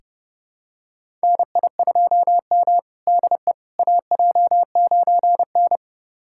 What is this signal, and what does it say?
Morse code, heard as a keyed tone, about 30 wpm, 700 Hz. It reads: NI2M DE AJ9N
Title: USA Callsign DE Callsign - 4 Characters - 30wpm